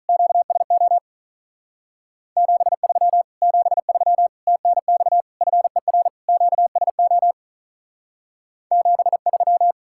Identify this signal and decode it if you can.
Morse code, heard as a keyed tone, about 44 wpm, 700 Hz. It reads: QSO 73 73 TNX FER QSO 73
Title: Speed-Racing Common QSO Phrases - 35wpm